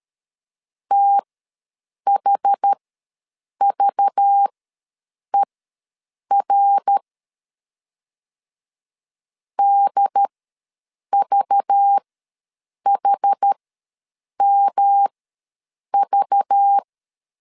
Morse code, heard as a keyed tone, about 13 wpm, spaced as for 7 wpm, 800 Hz. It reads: THVER DVHMV
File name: LEZIONI / 5 LBVC / 5 LBV ws 12wpm 3015